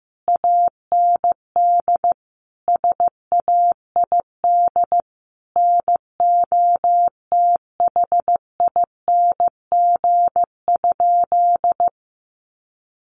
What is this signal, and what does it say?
Morse code, heard as a keyed tone, about 15 wpm, 700 Hz. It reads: AND SAID NOTHING?
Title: Sentences from Top 500 Words 15wpm